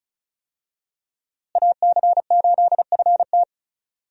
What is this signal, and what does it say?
Morse code, heard as a keyed tone, about 36 wpm, 700 Hz. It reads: AC8FT
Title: USA Call Signs - 5 Characters - Repeated 3x - 35wpm